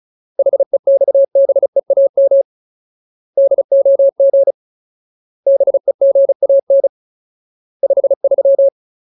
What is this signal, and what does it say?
Morse code, heard as a keyed tone, about 34 wpm, 550 Hz. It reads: HEXBEAM DOG BEGAN 53
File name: Ultimate Word List / 35WPM Ultimate